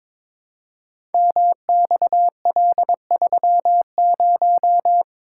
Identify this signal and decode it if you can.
Morse code, heard as a keyed tone, about 22 wpm, 700 Hz. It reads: MXL30
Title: Code Groups - 5 Letter:Numbers - 22wpm